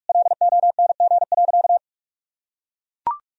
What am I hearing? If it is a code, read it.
Morse code, heard as a keyed tone, about 41 wpm, 700 Hz. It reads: LONG.
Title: Sentences from Top 1000 Words - Review Part 3 - 45wpm